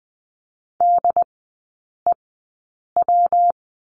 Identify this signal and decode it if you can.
Morse code, heard as a keyed tone, about 20 wpm, spaced as for 8 wpm, 700 Hz. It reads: DEW
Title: 114 Lesson 19e - Letter w - Three Characters 20wpm